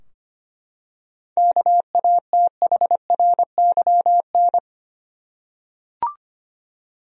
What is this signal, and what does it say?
Morse code, heard as a keyed tone, about 25 wpm, 700 Hz. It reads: KATHRYN
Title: USA Common Names - 25wpm